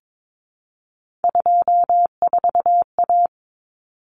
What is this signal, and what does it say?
Morse code, heard as a keyed tone, about 22 wpm, 700 Hz. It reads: 24A
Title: Common Cut Numbers - 3-Digit 22wpm